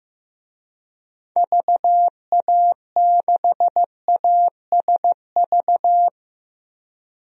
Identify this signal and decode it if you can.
Morse code, heard as a keyed tone, about 15 wpm, 700 Hz. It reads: VA6ASV
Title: DXCC-100 Call Signs - 6 Characters - Repeated 2x - 15wpm